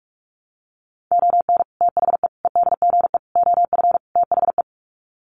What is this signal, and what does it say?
Morse code, heard as a keyed tone, about 44 wpm, 700 Hz. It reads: ON THE EDGE OF THE
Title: Sets of 5 Words 45wpm